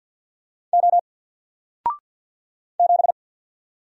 Morse code, heard as a keyed tone, about 50 wpm, 700 Hz. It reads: O 7
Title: Single Character (Letter:Number:Punctuation:Prosign) 50wpm